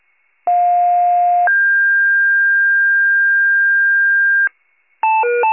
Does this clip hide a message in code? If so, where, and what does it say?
Yes: two-tone page from 0.5 to 4.5 s, heard as a one-second tone, then a three-second tone, about 710 then 1600 Hz.